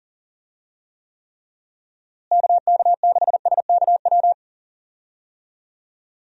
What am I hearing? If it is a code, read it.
Morse code, heard as a keyed tone, about 40 wpm, 700 Hz. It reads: KK6SKW
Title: USA Call Signs - 6 Characters - Repeated 3x - 40wpm